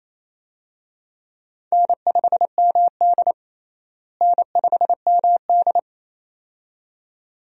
Morse code, heard as a keyed tone, about 28 wpm, 700 Hz. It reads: N5MD N5MD
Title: USA Call Signs - 4 Characters - Repeated 2x - 28wpm